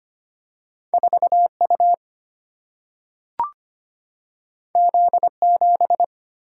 Morse code, heard as a keyed tone, about 25 wpm, 700 Hz. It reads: 4 U Z 7
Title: Single-Letter Single-Number 25wpm